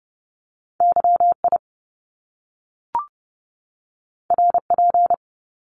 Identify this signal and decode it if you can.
Morse code, heard as a keyed tone, about 30 wpm, 700 Hz. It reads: YI RP
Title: Top Two Letter Combinations in Words 30wpm